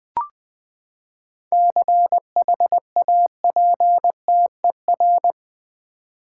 Morse code, heard as a keyed tone, about 20 wpm, 700 Hz. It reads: CHAPTER
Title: Seven Letter Words - 20wpm